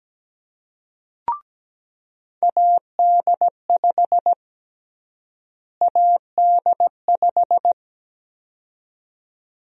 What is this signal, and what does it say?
Morse code, heard as a keyed tone, about 17 wpm, 700 Hz. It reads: AD5 AD5
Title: USA Call Sign Prefixes - Repeated 2x - 17wpm